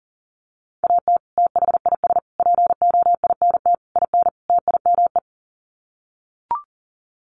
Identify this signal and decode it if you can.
Morse code, heard as a keyed tone, about 37 wpm, 700 Hz. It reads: AT THIS POINT IN TIME
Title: Sets of 5 Words Encore 40wpm